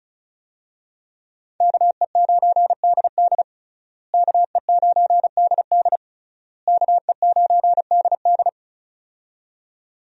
Morse code, heard as a keyed tone, about 36 wpm, 700 Hz. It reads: KE9DD KE9DD KE9DD
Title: USA Call Signs - 5 Characters - Repeated 3x - 35wpm